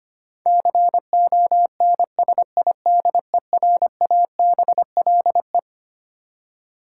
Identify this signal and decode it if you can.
Morse code, heard as a keyed tone, about 25 wpm, 700 Hz. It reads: CONSIDERABLE
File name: practice / 25wpm / Twelve Letter Words - 25wpm